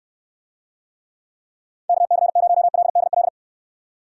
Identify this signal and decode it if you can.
Morse code, heard as a keyed tone, about 69 wpm, 700 Hz. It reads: KC0LDL